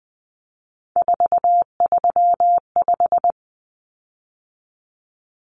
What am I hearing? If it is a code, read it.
Morse code, heard as a keyed tone, about 20 wpm, 700 Hz. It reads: 435
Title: Common Cut Numbers - 3-Digit 20wpm